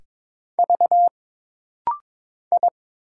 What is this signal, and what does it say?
Morse code, heard as a keyed tone, about 22 wpm, 700 Hz. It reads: V I
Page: This rapid-fire timing is geared to push your ICR (Instant Character Recognition) skills.